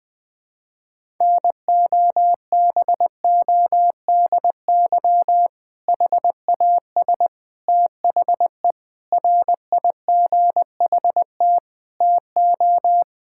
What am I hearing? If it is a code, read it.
Morse code, heard as a keyed tone, about 20 wpm, 700 Hz. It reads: NOBODY HAS THE RIGHT TO
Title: Sentences from Top 1600 Words 20wpm